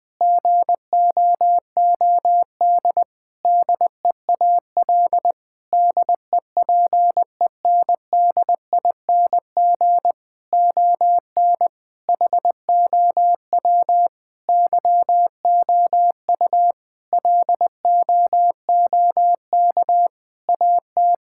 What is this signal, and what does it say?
Morse code, heard as a keyed tone, about 20 wpm, 700 Hz. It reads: GOOD DEAL DEPENDING ON HOW YOU LOOK AT
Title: Sentences with 7-Word Phrases - Spoken Beginning - 20wpm